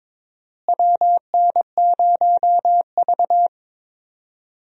Morse code, heard as a keyed tone, about 22 wpm, 700 Hz. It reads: WN0V
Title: USA Call Signs - 4 Characters - Repeated 2x - 22wpm